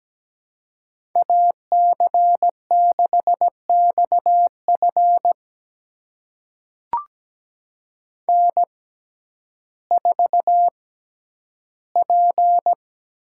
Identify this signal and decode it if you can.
Morse code, heard as a keyed tone, about 17 wpm, spaced as for 10 wpm, 700 Hz. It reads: AC6XF N4P
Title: USA Call Signs - 5 Characters - 6x Character Spacing - 17wpm